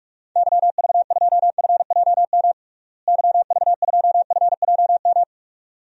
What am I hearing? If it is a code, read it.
Morse code, heard as a keyed tone, about 46 wpm, 700 Hz. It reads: YV2FJM YV2FJM
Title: DXCC-100 Call Signs - 6 Characters - Repeated 2x - 45wpm